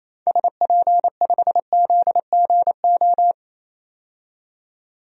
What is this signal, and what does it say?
Morse code, heard as a keyed tone, about 28 wpm, 700 Hz. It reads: SP5ZGO
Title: DXCC-100 Call Signs - 6 Characters - 28wpm